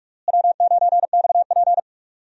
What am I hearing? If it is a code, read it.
Morse code, heard as a keyed tone, about 41 wpm, 700 Hz. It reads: W9XP
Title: USA Call Signs - 4 Characters - Repeated 3x - 45wpm